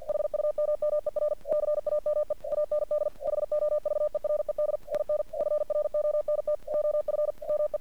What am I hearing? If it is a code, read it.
Morse code, heard as a keyed tone, about 48 wpm, 600 Hz. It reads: HUMMER CAME AND HOVERED IN FRONT OF ME
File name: archived / CRSnet